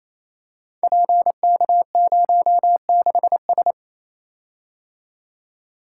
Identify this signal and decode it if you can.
Morse code, heard as a keyed tone, about 29 wpm, 700 Hz. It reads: PK06S
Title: Code Groups - 5 Letter:Numbers - 28wpm